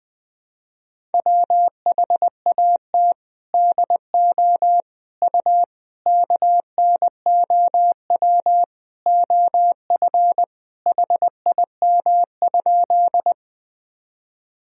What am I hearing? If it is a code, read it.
Morse code, heard as a keyed tone, about 20 wpm, 700 Hz. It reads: WHAT DO U KNOW OF HIM?
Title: Sentences from Top 100 Words and Abbreviations 20wpm